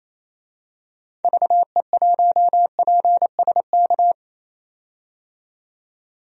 Morse code, heard as a keyed tone, about 28 wpm, 700 Hz. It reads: VE1PSK
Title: DXCC-100 Call Signs - 6 Characters - 28wpm